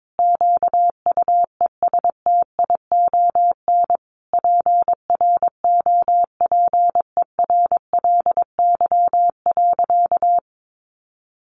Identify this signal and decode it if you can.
Morse code, heard as a keyed tone, about 22 wpm, 700 Hz. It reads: QUESTION PROPERLY.
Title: Sentences from Top 1100 Words 22wpm